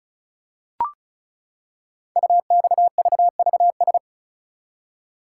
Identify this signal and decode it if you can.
Morse code, heard as a keyed tone, about 36 wpm, 690 Hz. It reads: UXVVS
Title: Code Groups - 5 Letters - 35wpm